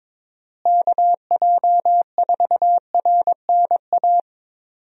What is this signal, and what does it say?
Morse code, heard as a keyed tone, about 22 wpm, 700 Hz. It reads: KJ4RNA
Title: USA Call Signs - 6 Characters - Repeated 2x - 22wpm